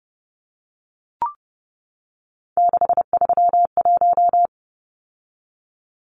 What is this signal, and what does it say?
Morse code, heard as a keyed tone, about 30 wpm, 700 Hz. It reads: 631
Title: Numbers - 3-Digit 30wpm